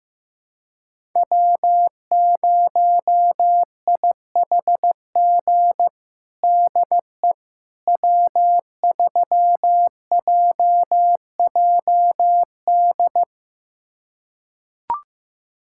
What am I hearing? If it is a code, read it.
Morse code, heard as a keyed tone, about 15 wpm, 700 Hz. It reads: W0IHG DE W3JJD
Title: USA Callsign DE Callsign - 5 Characters - 15wpm